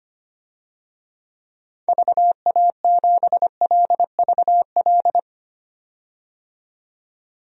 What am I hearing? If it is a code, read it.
Morse code, heard as a keyed tone, about 25 wpm, 700 Hz. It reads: VA7LVL
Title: DXCC-100 Call Signs - 6 Characters - 25wpm